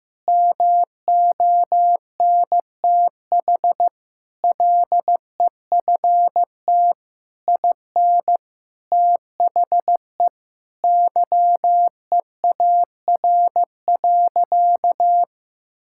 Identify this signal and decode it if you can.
Morse code, heard as a keyed tone, about 15 wpm, 700 Hz. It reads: MONTH LEFT IN THE YEAR.
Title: Sentences from Top 1000 Words - Review Part 5 - 15wpm